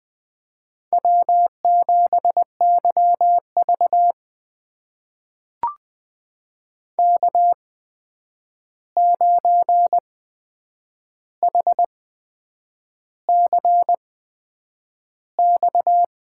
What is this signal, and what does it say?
Morse code, heard as a keyed tone, about 20 wpm, spaced as for 8 wpm, 700 Hz. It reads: W7YV K9HCX
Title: USA Call Signs - 8x Character Spacing - 20wpm